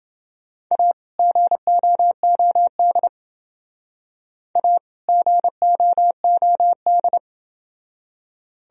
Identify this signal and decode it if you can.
Morse code, heard as a keyed tone, about 29 wpm, 700 Hz. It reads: A GOOD A GOOD